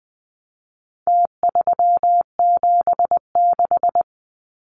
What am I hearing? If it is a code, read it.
Morse code, heard as a keyed tone, about 20 wpm, 700 Hz. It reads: T376
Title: Numbers - Contest Style - 4-Digit 20wpm